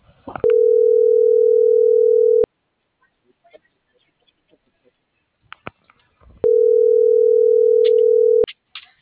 Ringback tone (North America), 440 + 480 Hz.